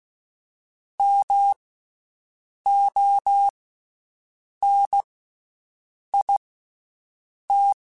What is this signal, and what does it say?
Morse code, heard as a keyed tone, about 16 wpm, spaced as for 6 wpm, 800 Hz. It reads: MONIT
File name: Special INGs-mp3s / Special INGs-mp3s / Slow ing @15